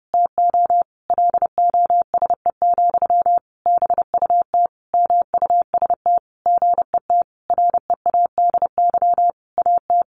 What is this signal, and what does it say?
Morse code, heard as a keyed tone, about 28 wpm, 700 Hz. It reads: TO LOSE, BUT MUST GET READY AT
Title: Sentences from Top 1100 Words 30wpm